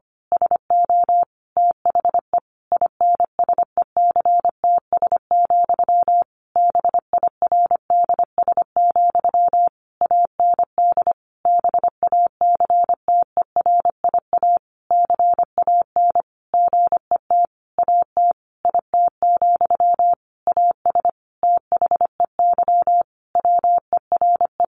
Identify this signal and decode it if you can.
Morse code, heard as a keyed tone, about 25 wpm, 700 Hz. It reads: SO THE INSECTS, BIRDS, AND BACTERIA CAN GET AT IT, AS THEY WERE
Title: A Sound Of Thunder 25wpm